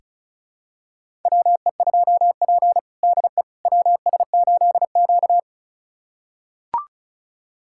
Morse code, heard as a keyed tone, about 34 wpm, 700 Hz. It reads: WE2P DE WS8Q
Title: USA Callsign DE Callsign - 4 Characters - 35wpm